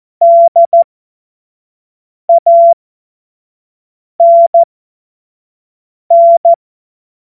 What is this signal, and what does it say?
Morse code, heard as a keyed tone, about 14 wpm, spaced as for 5 wpm, 680 Hz. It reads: DANN
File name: Morsekurs / Witze / Witz 02. November 2021, 25 F70